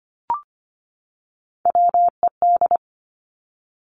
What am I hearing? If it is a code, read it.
Morse code, heard as a keyed tone, about 25 wpm, 700 Hz. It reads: WED
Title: Three Challenging Letter-Number Combinations 25wpm